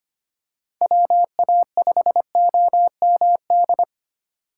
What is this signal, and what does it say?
Morse code, heard as a keyed tone, about 25 wpm, 700 Hz. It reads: WA5OMD
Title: USA Call Signs - 6 Characters - 8x Character Spacing - 25wpm